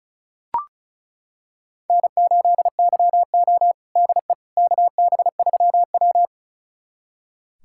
Morse code, heard as a keyed tone, about 36 wpm, 700 Hz. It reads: N8YO DE KB3W
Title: USA Callsign DE Callsign - 4 Characters - 35wpm